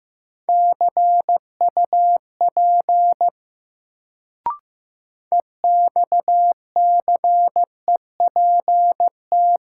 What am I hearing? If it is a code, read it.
Morse code, heard as a keyed tone, about 15 wpm, 700 Hz. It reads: CUP EXCEPT
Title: Sentences from Top 1600 Words - New Word Drill Alt - 15wpm